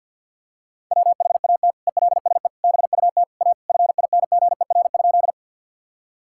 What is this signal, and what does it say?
Morse code, heard as a keyed tone, about 50 wpm, 700 Hz. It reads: WHAT ELSE BUT A FINGER?
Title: Sentences from Top 1600 Words 50wpm